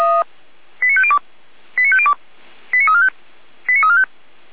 Five-tone selective call (ZVEI), heard as quick runs of five tones, tones of about 70 ms. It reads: 89502 89502 89345 89345